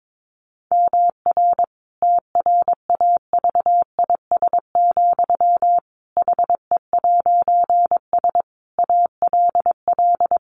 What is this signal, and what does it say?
Morse code, heard as a keyed tone, about 22 wpm, 700 Hz. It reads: MR TRAVIS, HE'S ALL
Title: A Sound Of Thunder 22wpm